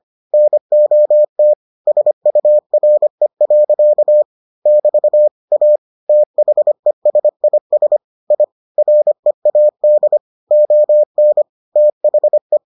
Morse code, heard as a keyed tone, about 25 wpm, 600 Hz. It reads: NOT SURE. <BT> A THESIS I READ ON THE